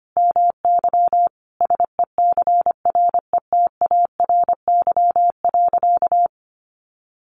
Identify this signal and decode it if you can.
Morse code, heard as a keyed tone, about 25 wpm, 700 Hz. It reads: MY SECRETARY.
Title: Sentences from Top 1300 Words 25wpm